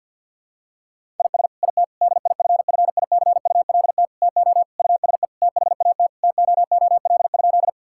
Morse code, heard as a keyed tone, about 50 wpm, 700 Hz. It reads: IS IT DIFFICULT TO USE THAT TOOL?
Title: Sentences from Top 1000 Words - Review Part 6 - 50wpm